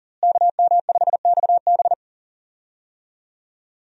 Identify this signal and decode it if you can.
Morse code, heard as a keyed tone, about 40 wpm, 700 Hz. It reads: KM5XB
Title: USA Call Signs - 5 Characters - Repeated 3x - 40wpm